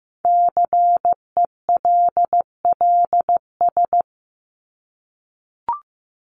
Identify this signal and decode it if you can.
Morse code, heard as a keyed tone, about 15 wpm, 700 Hz. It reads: CELLS
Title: ES:S Suffix - Top 100 Words with Base Words 15wpm